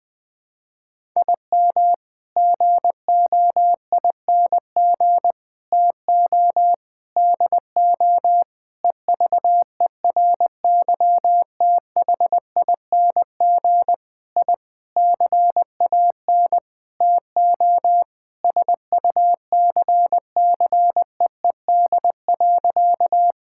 Morse code, heard as a keyed tone, about 20 wpm, 700 Hz. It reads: IM GOING TO DO EVERYTHING I CAN TO SUCCEED.
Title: Sentences with 8-Word Phrases - 20wpm